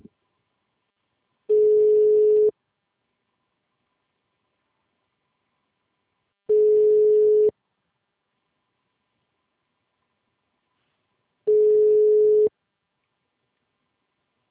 Ringback tone (Europe, 425 Hz).